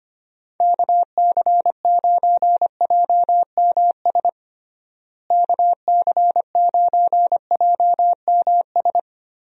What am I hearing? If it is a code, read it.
Morse code, heard as a keyed tone, about 25 wpm, 700 Hz. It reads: KC9JMS KC9JMS
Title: USA Call Signs - 6 Characters - Repeated 2x - 25wpm